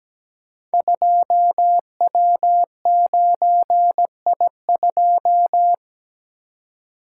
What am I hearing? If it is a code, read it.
Morse code, heard as a keyed tone, about 17 wpm, 700 Hz. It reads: 2W9I2
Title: Code Groups - 5 Letter:Numbers - 17wpm